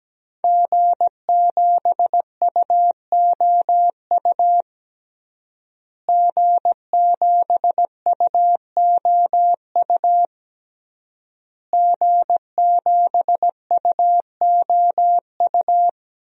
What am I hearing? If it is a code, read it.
Morse code, heard as a keyed tone, about 17 wpm, 700 Hz. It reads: G7UOU G7UOU G7UOU
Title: DXCC-100 Call Signs - 5 Characters - Repeated 3x - 17wpm